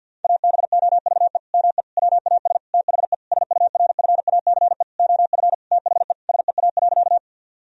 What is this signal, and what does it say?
Morse code, heard as a keyed tone, about 52 wpm, 700 Hz. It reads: ABOVE ME WAS THE SURFACE OF THE SEA.
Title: Sentences from Top 1000 Words - Review Part 3 - 50wpm